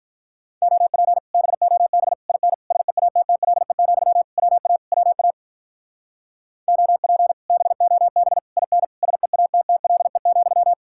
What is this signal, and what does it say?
Morse code, heard as a keyed tone, about 45 wpm, 700 Hz. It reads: OP BOB IN SEATTLE, WA WA OP BOB IN SEATTLE,